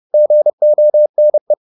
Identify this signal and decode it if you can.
Morse code, heard as a keyed tone, about 30 wpm, 600 Hz. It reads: GONE
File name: the old man and the sea / the old man and the sea 1 30wpm